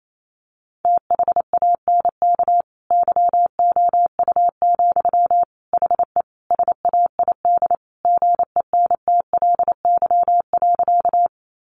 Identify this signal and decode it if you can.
Morse code, heard as a keyed tone, about 28 wpm, 700 Hz. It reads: THANK YOU, HE SAID GENTLY.
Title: Sentences from Top 1500 Words 28wpm